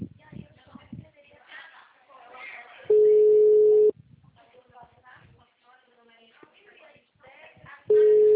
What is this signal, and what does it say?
Ringback tone (Europe, 425 Hz), with speech.